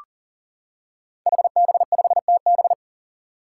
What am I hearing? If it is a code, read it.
Morse code, heard as a keyed tone, about 40 wpm, 700 Hz. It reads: HB5TB
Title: Five Challenging Letter-Number Combinations 40wpm